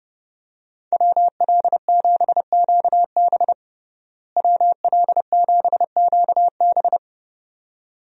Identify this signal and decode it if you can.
Morse code, heard as a keyed tone, about 30 wpm, 700 Hz. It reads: WL7QB WL7QB
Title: USA Call Signs - 5 Characters - Repeated 2x - 30wpm